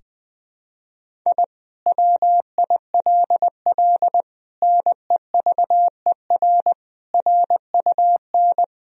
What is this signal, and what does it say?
Morse code, heard as a keyed tone, about 21 wpm, 700 Hz. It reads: I WILL NEVER RUN
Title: Sentences from Top 1000 Words - Review Part 1 - 20wpm